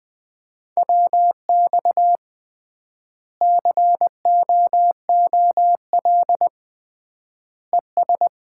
Morse code, heard as a keyed tone, about 21 wpm, 700 Hz. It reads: WX COOL ES
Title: Common QSO Phrases - 3x Word-Spacing - 20wpm